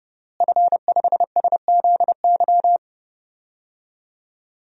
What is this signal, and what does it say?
Morse code, heard as a keyed tone, about 32 wpm, 700 Hz. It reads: F5SZY
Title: DXCC-100 Call Signs - 5 Characters - Repeated 3x - 30wpm